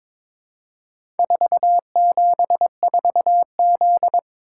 Morse code, heard as a keyed tone, about 23 wpm, 700 Hz. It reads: 474Z
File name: practice / 22wpm / Four Letter-Number Combinations - 22wpm